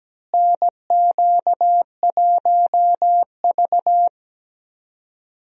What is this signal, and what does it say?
Morse code, heard as a keyed tone, about 17 wpm, 700 Hz. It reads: NQ1V
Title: USA Call Signs - 4 Characters - Repeated 3x - 17wpm